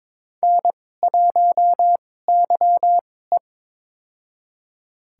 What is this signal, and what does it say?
Morse code, heard as a keyed tone, about 22 wpm, spaced as for 16 wpm, 700 Hz. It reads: N1YE
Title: USA Call Signs - 4 Characters - 2x Character Spacing - 22wpm